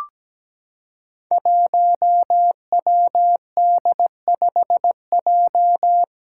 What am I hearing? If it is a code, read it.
Morse code, heard as a keyed tone, about 17 wpm, 700 Hz. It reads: 1WD5J
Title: Five Challenging Letter-Number Combinations 17wpm